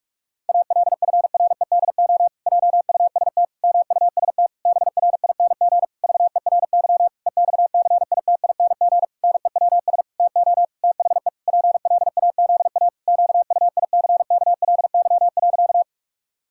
Morse code, heard as a keyed tone, about 46 wpm, 700 Hz. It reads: ALFREDO JUST MUST BRING VERY EXCITING NEWS TO THE PLAZA QUICKLY.
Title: Pangram Challenge v2 - 45wpm